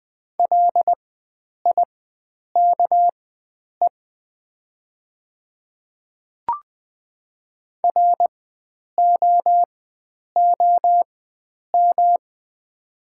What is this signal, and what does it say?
Morse code, heard as a keyed tone, about 21 wpm, spaced as for 9 wpm, 700 Hz. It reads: LIKE ROOM
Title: 4x Character Spacing - Top 500 Words - 20wpm